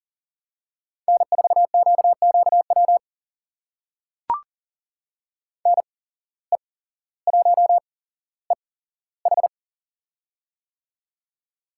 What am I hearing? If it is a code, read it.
Morse code, heard as a keyed tone, about 39 wpm, spaced as for 17 wpm, 700 Hz. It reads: N4QQW NE1EH
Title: USA Call Signs - 5 Characters - 8x Character Spacing - 40wpm